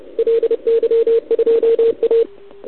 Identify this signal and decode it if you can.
Morse code, heard as a keyed tone, about 30 wpm, 460 Hz, with radio noise. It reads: LY2A
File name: ARCHIV / OKDX2019 / OKOM40 KX3